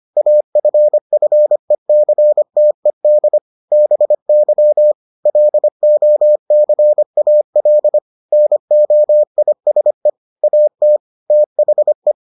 Morse code, heard as a keyed tone, about 25 wpm, 600 Hz. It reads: AFFECTED BY LOCAL NOISE AT THE